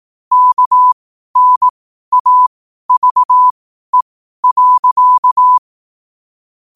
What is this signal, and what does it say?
Morse code, heard as a keyed tone, about 18 wpm, spaced as for 12 wpm, 1000 Hz. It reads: KNAVE.
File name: qotdfiles / 12wpm